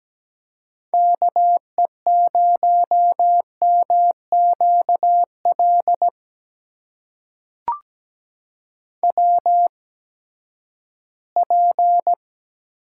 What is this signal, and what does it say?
Morse code, heard as a keyed tone, about 17 wpm, spaced as for 12 wpm, 700 Hz. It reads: KE0MQL WP
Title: USA Call Signs - 6 Characters - 8x Character Spacing - 17wpm